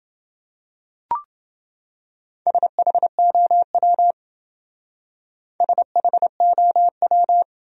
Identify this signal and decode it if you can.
Morse code, heard as a keyed tone, about 28 wpm, 700 Hz. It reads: SHOW SHOW